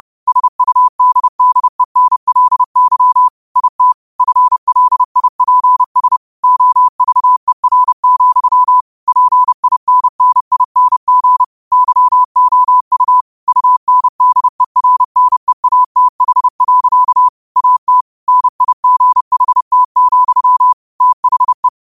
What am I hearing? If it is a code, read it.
Morse code, heard as a keyed tone, about 30 wpm, 1000 Hz. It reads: SUDDENLY IT FLIPS OVER, PINNING YOU UNDERNEATH. AT NIGHT, THE